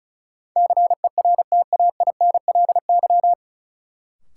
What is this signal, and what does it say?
Morse code, heard as a keyed tone, about 34 wpm, 700 Hz. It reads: CERTAINLY
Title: Sentences from Top 900 Words - New Word Drill - 35wpm